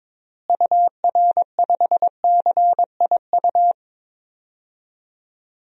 Morse code, heard as a keyed tone, about 22 wpm, 700 Hz. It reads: UR5CIU
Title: DXCC-100 Call Signs - 6 Characters - Repeated 3x - 22wpm